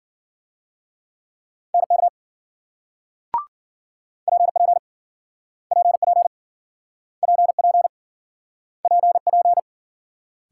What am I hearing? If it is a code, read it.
Morse code, heard as a keyed tone, about 57 wpm, 700 Hz. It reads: NK PP PP PP PP